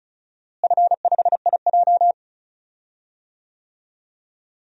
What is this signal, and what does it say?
Morse code, heard as a keyed tone, about 36 wpm, 700 Hz. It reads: F5IJ
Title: DXCC-100 Call Signs - 4 Characters - Repeated 2x - 35wpm